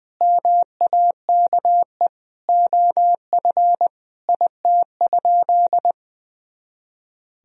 Morse code, heard as a keyed tone, about 21 wpm, 700 Hz. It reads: MAKE OF IT?
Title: Sentences from Top 100 Words and Abbreviations 20wpm